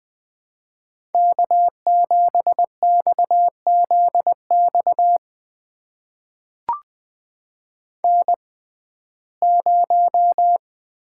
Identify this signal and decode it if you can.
Morse code, heard as a keyed tone, about 20 wpm, spaced as for 15 wpm, 700 Hz. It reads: K7XZX N0
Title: USA Call Signs - 5 Characters - 6x Character Spacing - 20wpm